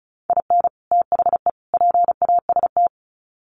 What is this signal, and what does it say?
Morse code, heard as a keyed tone, about 34 wpm, 700 Hz. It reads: IN THE PAST
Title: Sets of 3 Words 35wpm